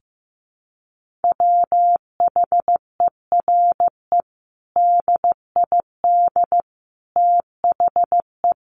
Morse code, heard as a keyed tone, about 15 wpm, 700 Hz. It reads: WHERE DID THE
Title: Sentences from Top 1100 Words 15wpm